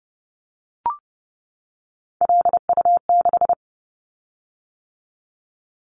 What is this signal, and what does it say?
Morse code, heard as a keyed tone, about 30 wpm, 700 Hz. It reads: LU6
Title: Callsign Prefixes for Top 100 DXCC - 30wpm